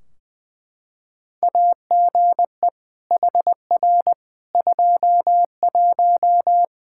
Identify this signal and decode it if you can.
Morse code, heard as a keyed tone, about 21 wpm, 700 Hz. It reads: AGE HR 21
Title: K1USN SST QSO Practice 20wpm